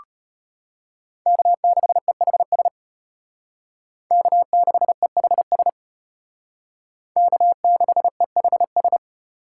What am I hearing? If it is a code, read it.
Morse code, heard as a keyed tone, about 33 wpm, 700 Hz. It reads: K6EHS K6EHS K6EHS